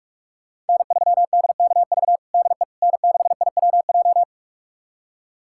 Morse code, heard as a keyed tone, about 47 wpm, 700 Hz. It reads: N3DKV DE N6IWJ
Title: USA Callsign DE Callsign - 5 Characters - 45wpm